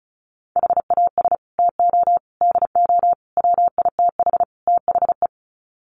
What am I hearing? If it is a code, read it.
Morse code, heard as a keyed tone, about 34 wpm, 700 Hz. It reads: HAS TO DO WITH THE
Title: Sets of 5 Words 35wpm